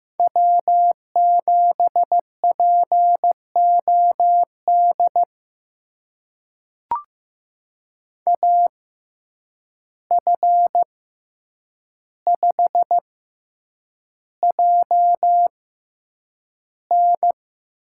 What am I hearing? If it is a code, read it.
Morse code, heard as a keyed tone, about 15 wpm, spaced as for 8 wpm, 700 Hz. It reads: W7POD AF5JN
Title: USA Call Signs - 5 Characters - 6x Character Spacing - 15wpm